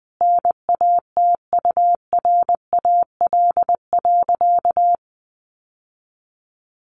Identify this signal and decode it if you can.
Morse code, heard as a keyed tone, about 19 wpm, 700 Hz. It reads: NATURAL.
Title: Sentences from Top 700 Words 20wpm